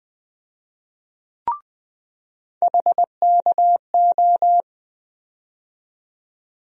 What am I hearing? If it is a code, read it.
Morse code, heard as a keyed tone, about 20 wpm, 700 Hz. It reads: HKO
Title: USA Call Sign Postfixes - 3 Letters 20wpm